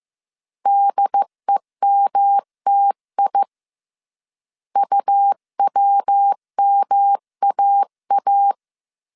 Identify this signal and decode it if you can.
Morse code, heard as a keyed tone, about 15 wpm, 800 Hz. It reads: DEMTI UWMAA
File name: CW / DET / 1 3 1500C S A 14wpm 2447